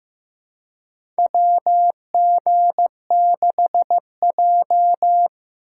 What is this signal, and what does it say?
Morse code, heard as a keyed tone, about 15 wpm, 700 Hz. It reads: WG6J
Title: USA Call Signs - 4 Characters - 8x Character Spacing - 15wpm